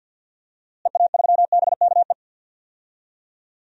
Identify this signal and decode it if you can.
Morse code, heard as a keyed tone, about 53 wpm, 700 Hz. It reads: EA3BKE
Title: DXCC-100 Call Signs - 6 Characters - Repeated 3x - 50wpm